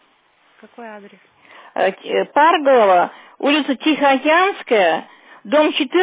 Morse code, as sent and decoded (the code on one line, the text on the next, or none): none